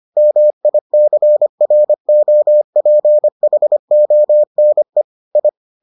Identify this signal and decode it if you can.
Morse code, heard as a keyed tone, about 25 wpm, 600 Hz. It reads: MICROPHONE I